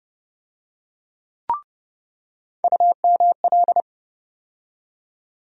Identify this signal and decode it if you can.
Morse code, heard as a keyed tone, about 30 wpm, 700 Hz. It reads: UML
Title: USA Call Sign Postfixes - 3 Letters 30wpm